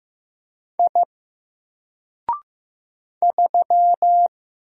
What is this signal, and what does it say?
Morse code, heard as a keyed tone, about 15 wpm, 700 Hz. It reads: I 3
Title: Single Character (Letter:Number:Punctuation:Prosign) 15wpm